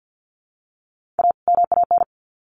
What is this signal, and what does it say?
Morse code, heard as a keyed tone, about 50 wpm, 700 Hz. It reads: A MAN
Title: Sets of 2 Words 50wpm